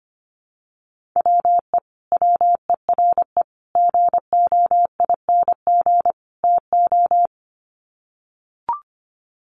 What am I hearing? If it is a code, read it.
Morse code, heard as a keyed tone, about 24 wpm, 700 Hz. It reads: WE WERE GOING TO
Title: Sets of 4 Words 25wpm